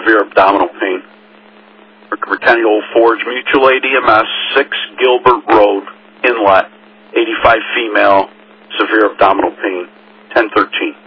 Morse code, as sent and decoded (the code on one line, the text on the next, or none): none